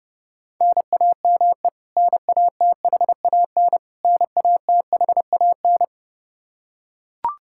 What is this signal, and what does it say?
Morse code, heard as a keyed tone, about 32 wpm, 700 Hz. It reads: NAME NATHAN NATHAN
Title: Practice QSO - Exchange 1 - Beginning - 30wpm